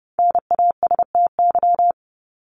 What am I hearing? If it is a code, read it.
Morse code, heard as a keyed tone, about 30 wpm, 700 Hz. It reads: NASTY
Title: Top 1751-2000 Words 30wpm